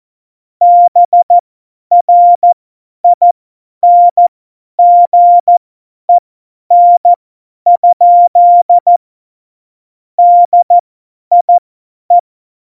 Morse code, heard as a keyed tone, about 14 wpm, spaced as for 10 wpm, 700 Hz. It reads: BRINGEN? DIE
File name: Morsekurs / Witze / Witz 15. September 2020, 50 F70